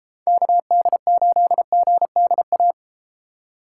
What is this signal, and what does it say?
Morse code, heard as a keyed tone, about 34 wpm, 700 Hz. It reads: KD8GDA